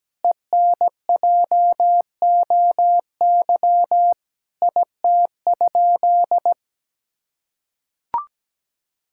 Morse code, heard as a keyed tone, about 17 wpm, 700 Hz. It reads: ENJOY IT?
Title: Sentences from Top 1000 Words - Review Part 5 - 17wpm